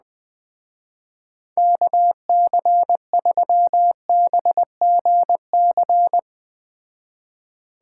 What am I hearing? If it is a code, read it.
Morse code, heard as a keyed tone, about 20 wpm, 700 Hz. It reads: KC3BGC